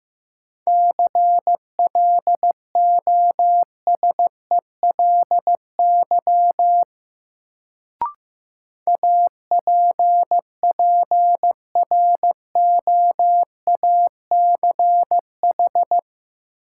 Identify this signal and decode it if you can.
Morse code, heard as a keyed tone, about 15 wpm, 700 Hz. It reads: CLOSELY APPROACH
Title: Sentences from Top 1400 Words - New Word Drill Alt - 15wpm